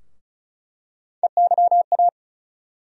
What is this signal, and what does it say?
Morse code, heard as a keyed tone, about 34 wpm, 700 Hz. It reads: EYA
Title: USA Call Sign Postfixes - 3 Letters 35wpm